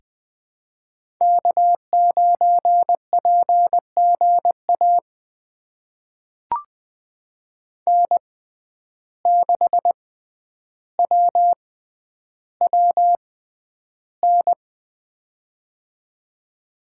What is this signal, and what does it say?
Morse code, heard as a keyed tone, about 20 wpm, spaced as for 10 wpm, 700 Hz. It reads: K9PGA N6WWN
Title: USA Call Signs - 5 Characters - 6x Character Spacing - 20wpm